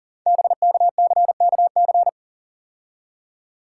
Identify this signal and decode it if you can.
Morse code, heard as a keyed tone, about 39 wpm, 700 Hz. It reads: BKCKC
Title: Code Groups - 5 Letters - 40wpm